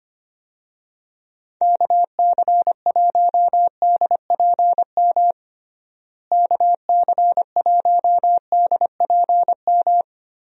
Morse code, heard as a keyed tone, about 25 wpm, 700 Hz. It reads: KC1DPM KC1DPM
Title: USA Call Signs - 6 Characters - Repeated 3x - 25wpm